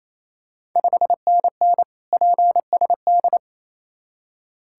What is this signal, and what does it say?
Morse code, heard as a keyed tone, about 28 wpm, 690 Hz. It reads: 5NN PSD
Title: Ontario QSO Party Exchange 28wpm